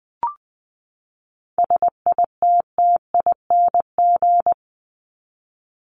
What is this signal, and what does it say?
Morse code, heard as a keyed tone, about 20 wpm, 700 Hz. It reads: SITTING
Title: ING Suffix - Top 100 Words with Base Words 20wpm